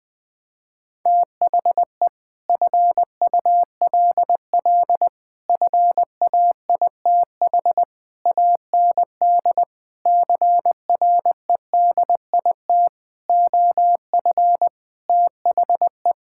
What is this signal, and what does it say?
Morse code, heard as a keyed tone, about 21 wpm, 700 Hz. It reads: THE FULL FAITH AND CREDIT OF THE
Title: Sentences with 8-Word Phrases - 20wpm